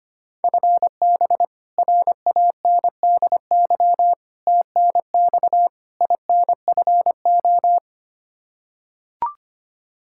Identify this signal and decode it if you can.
Morse code, heard as a keyed tone, about 25 wpm, 700 Hz. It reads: FB RANDY TNX INFO
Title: K1USN SST QSO Practice 25wpm